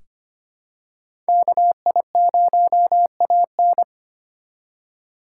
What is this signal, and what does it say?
Morse code, heard as a keyed tone, about 25 wpm, 700 Hz. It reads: KI0AN
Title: USA Call Signs - 5 Characters - 4x Character Spacing - 25wpm